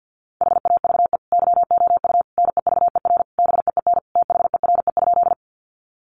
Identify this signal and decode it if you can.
Morse code, heard as a keyed tone, about 50 wpm, 700 Hz. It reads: HAVE YOU NEVER BEEN THERE?
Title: Sentences from Top 300 Words 50wpm